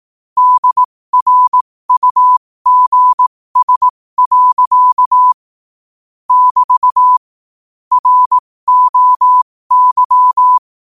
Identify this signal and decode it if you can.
Morse code, heard as a keyed tone, about 18 wpm, spaced as for 15 wpm, 1000 Hz. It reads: DRUGS. <BT> ROY
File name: qotdfiles / 15wpm